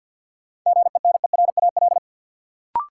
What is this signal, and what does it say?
Morse code, heard as a keyed tone, about 53 wpm, 700 Hz. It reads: GENERAL
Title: Sentences from Top 600 Words - New Word Drill - 50wpm